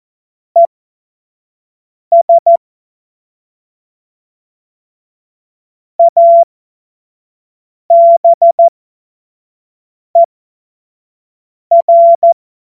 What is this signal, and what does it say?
Morse code, heard as a keyed tone, about 14 wpm, spaced as for 5 wpm, 680 Hz. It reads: ES ABER
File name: Morsekurs / Witze / Witz 03. Mai 2022 25 F70